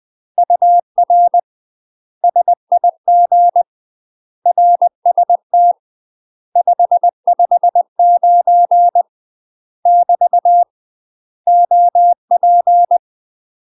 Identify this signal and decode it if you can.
Morse code, heard as a keyed tone, about 20 wpm, 700 Hz. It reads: UR SIG RST 559 <BT> OP
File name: UK MASTER QSOs TRAINING / 20WPM / UK MASTER QSOs 0002